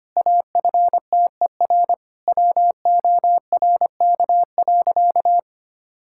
Morse code, heard as a keyed tone, about 25 wpm, 700 Hz. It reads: AFTER WORK.
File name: practice / 25wpm / Difficult Sentences with Hints - 25wpm